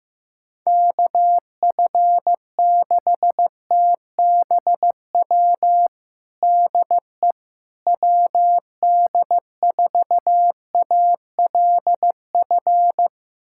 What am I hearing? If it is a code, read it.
Morse code, heard as a keyed tone, about 16 wpm, 700 Hz. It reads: KF6TBW DE WD4ALF
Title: USA Callsign DE Callsign - 6 Characters - 15wpm